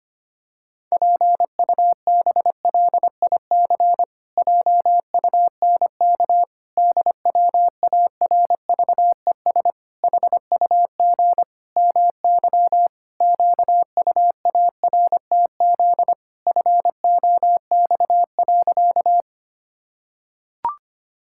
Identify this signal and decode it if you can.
Morse code, heard as a keyed tone, about 25 wpm, 700 Hz. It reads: PUBLIC JUNK DWARVES HUG MY QUARTZ FOX.
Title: Pangram Challenge v2 - 25wpm